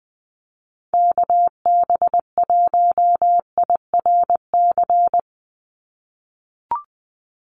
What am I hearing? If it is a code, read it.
Morse code, heard as a keyed tone, about 20 wpm, 700 Hz. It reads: KB1IRC
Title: USA Call Signs 20wpm